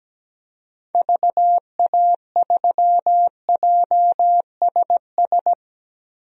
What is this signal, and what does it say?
Morse code, heard as a keyed tone, about 17 wpm, 700 Hz. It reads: VA3JSS
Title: DXCC-100 Call Signs - 6 Characters - 17wpm